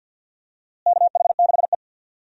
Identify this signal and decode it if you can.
Morse code, heard as a keyed tone, about 51 wpm, 700 Hz. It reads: KH6E
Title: DXCC-100 Call Signs - 4 Characters - Repeated 2x - 50wpm